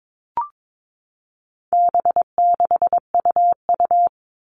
Morse code, heard as a keyed tone, about 22 wpm, 700 Hz. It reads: B6UU
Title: All Cut Numbers - 4-Digit 22wpm